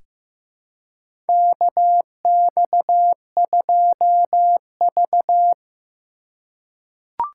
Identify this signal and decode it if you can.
Morse code, heard as a keyed tone, about 15 wpm, 700 Hz. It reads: KX2V
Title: USA Call Signs - 3x Character Spacing - 15wpm